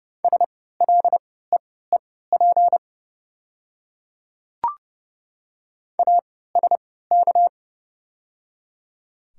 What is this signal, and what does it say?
Morse code, heard as a keyed tone, about 30 wpm, spaced as for 17 wpm, 700 Hz. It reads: SLEEP ASK
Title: 3x Character Spacing - Top 500 Words - 30wpm